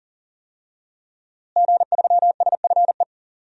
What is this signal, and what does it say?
Morse code, heard as a keyed tone, about 40 wpm, 700 Hz. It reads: G3SFE
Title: DXCC-100 Call Signs - 5 Characters - Repeated 2x - 40wpm